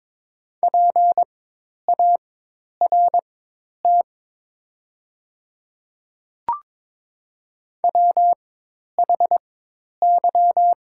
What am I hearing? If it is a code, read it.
Morse code, heard as a keyed tone, about 22 wpm, spaced as for 10 wpm, 700 Hz. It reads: PART WHY
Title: 4x Character Spacing - Top 200 Words - 22wpm